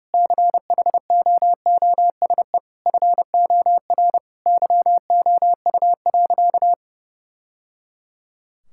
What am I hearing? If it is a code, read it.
Morse code, heard as a keyed tone, about 30 wpm, 700 Hz. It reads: CHOOSE FOR YOU.
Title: Sentences from Top 1000 Words - Review Part 6 - 30wpm